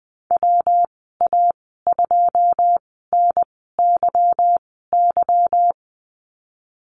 Morse code, heard as a keyed tone, about 20 wpm, spaced as for 15 wpm, 700 Hz. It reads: WA2NYY
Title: 154 Lesson 25g - Letter p - Call Signs 20wpm